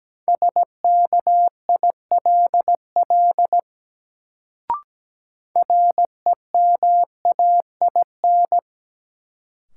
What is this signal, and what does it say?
Morse code, heard as a keyed tone, about 17 wpm, 700 Hz. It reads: SKILL REMAIN
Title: Sentences from Top 1200 Words - New Word Drill - 17wpm